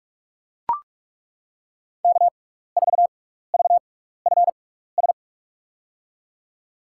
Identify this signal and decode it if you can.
Morse code, heard as a keyed tone, about 41 wpm, spaced as for 15 wpm, 690 Hz. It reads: K4VFS
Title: USA Call Signs - 5 Characters - 6x Character Spacing - 45wpm